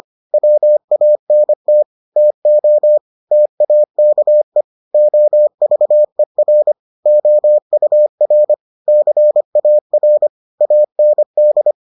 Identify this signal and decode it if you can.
Morse code, heard as a keyed tone, about 25 wpm, 600 Hz. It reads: WANT TO TAKE OVER OUR CAR AND